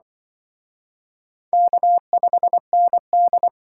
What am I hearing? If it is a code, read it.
Morse code, heard as a keyed tone, about 25 wpm, 700 Hz. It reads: K5ND
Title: Speed Racing - USA Call Signs 20wpm